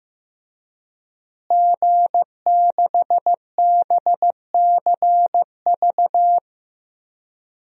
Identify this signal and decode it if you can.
Morse code, heard as a keyed tone, about 15 wpm, 700 Hz. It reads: G6BCV
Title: DXCC-100 Call Signs - 5 Characters - Repeated 3x - 15wpm